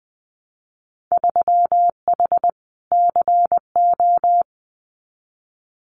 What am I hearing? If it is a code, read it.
Morse code, heard as a keyed tone, about 20 wpm, 700 Hz. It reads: 3H CO
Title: Winter Field Day Exchanges 20wpm